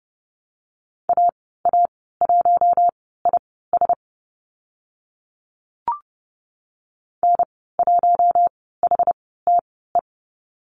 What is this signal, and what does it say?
Morse code, heard as a keyed tone, about 30 wpm, spaced as for 17 wpm, 700 Hz. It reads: AA1IS N1HTE
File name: Lesson 14g - Number 2 - Call Signs 30wpm